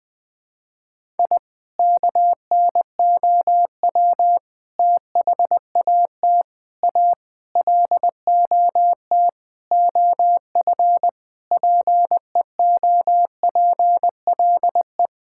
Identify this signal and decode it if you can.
Morse code, heard as a keyed tone, about 20 wpm, 700 Hz. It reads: I KNOW THAT A LOT OF PEOPLE
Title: Sentences with 4-Word Phrases - 20wpm